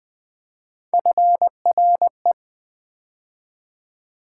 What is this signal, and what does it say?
Morse code, heard as a keyed tone, about 20 wpm, 700 Hz. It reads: FRE
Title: Build A Word - 3 Letters At A Time - 6-Letter Words - 20wpm